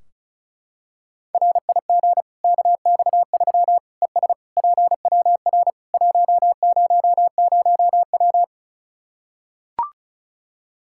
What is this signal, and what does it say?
Morse code, heard as a keyed tone, about 36 wpm, 700 Hz. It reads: RIG KX3 ES PWR 100W
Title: K1USN SST QSO Practice 35wpm